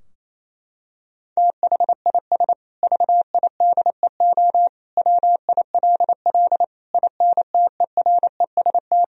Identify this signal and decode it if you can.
Morse code, heard as a keyed tone, about 27 wpm, 700 Hz. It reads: THIS VIDEO WILL INTEREST
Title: Sentences from Top 1000 Words - Review Part 4 - 28wpm